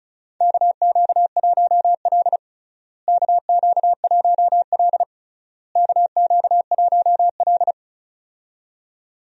Morse code, heard as a keyed tone, about 34 wpm, 700 Hz. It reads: KQ1L KQ1L KQ1L
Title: USA Call Signs - 4 Characters - Repeated 3x - 35wpm